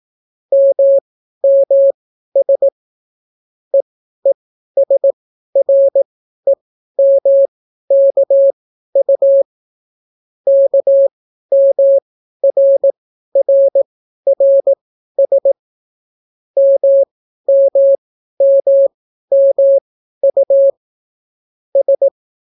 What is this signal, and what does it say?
Morse code, heard as a keyed tone, about 18 wpm, spaced as for 12 wpm, 550 Hz. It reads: MMS EESREMKU KMRRRS MMMMU S